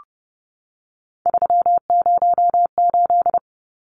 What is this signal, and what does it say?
Morse code, heard as a keyed tone, about 30 wpm, 700 Hz. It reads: 308
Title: Numbers - 3-Digit 30wpm